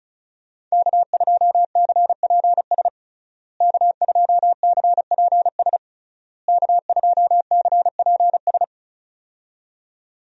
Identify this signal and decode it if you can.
Morse code, heard as a keyed tone, about 36 wpm, 700 Hz. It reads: K2CPS K2CPS K2CPS
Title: USA Call Signs - 5 Characters - Repeated 3x - 35wpm